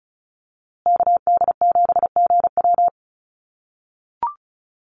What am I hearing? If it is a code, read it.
Morse code, heard as a keyed tone, about 34 wpm, 700 Hz. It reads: KD7GW
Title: USA Call Signs Repeated 2x 35wpm